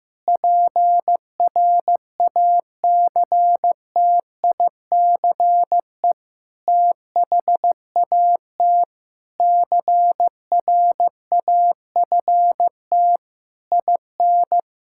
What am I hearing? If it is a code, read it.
Morse code, heard as a keyed tone, about 15 wpm, 700 Hz. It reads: PRACTICE THAT CRAFT IN
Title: Sentences from Top 1000 Words - Review Part 6 - 15wpm